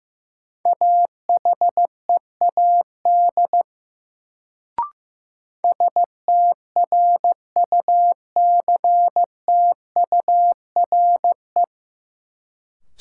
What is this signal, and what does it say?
Morse code, heard as a keyed tone, about 15 wpm, 700 Hz. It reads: AHEAD STRUCTURE
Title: Sentences from Top 1000 Words - Review Part 6 - New Word Drill - 15wpm